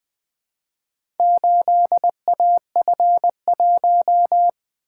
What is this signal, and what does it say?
Morse code, heard as a keyed tone, about 20 wpm, 700 Hz. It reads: 8AF1
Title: Four Letter-Number Combinations - 20wpm